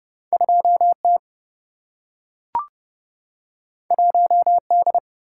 Morse code, heard as a keyed tone, about 30 wpm, 700 Hz. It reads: 2T 1D
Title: Single-Letter Single-Number 30wpm